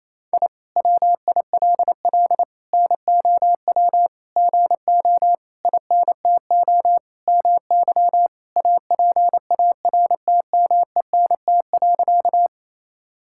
Morse code, heard as a keyed tone, about 27 wpm, 700 Hz. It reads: I WILL NOW GO INTO MY APARTMENT.
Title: Sentences from Top 1000 Words - Review Part 6 - 28wpm